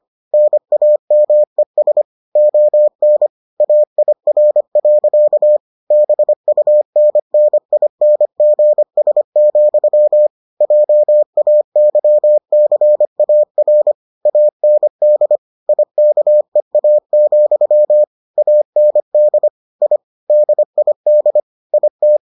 Morse code, heard as a keyed tone, about 25 wpm, 600 Hz. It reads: NAMES ON AIR. BUNNINGS, JAYCAR AND IKEA, AND I DID IT